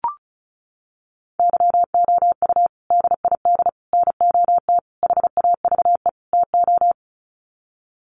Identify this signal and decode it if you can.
Morse code, heard as a keyed tone, about 34 wpm, 700 Hz. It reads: YOU DID NOT HAVE TO
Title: Sets of 5 Words Encore 35wpm